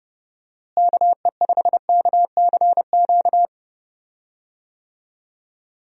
Morse code, heard as a keyed tone, about 30 wpm, 700 Hz. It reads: KE5KCQ
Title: USA Call Signs - 6 Characters - Repeated 2x - 30wpm